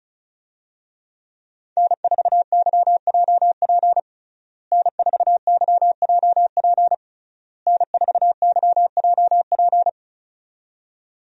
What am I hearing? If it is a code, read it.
Morse code, heard as a keyed tone, about 34 wpm, 700 Hz. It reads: N4YJP N4YJP N4YJP
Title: USA Call Signs - 5 Characters - Repeated 3x - 35wpm